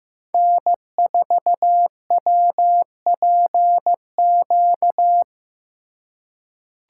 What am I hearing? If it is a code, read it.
Morse code, heard as a keyed tone, about 15 wpm, 700 Hz. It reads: N4WPQ
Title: USA Call Signs - 5 Characters - 3x Character Spacing - 15wpm